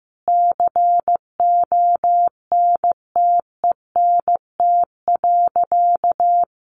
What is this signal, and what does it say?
Morse code, heard as a keyed tone, about 15 wpm, 700 Hz. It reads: CONTENT.
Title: Sentences from Top 900 Words 15wpm